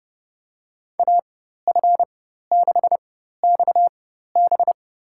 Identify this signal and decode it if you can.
Morse code, heard as a keyed tone, about 30 wpm, spaced as for 14 wpm, 700 Hz. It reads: AF6XB
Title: USA Call Signs - 5 Characters - 4x Character Spacing - 30wpm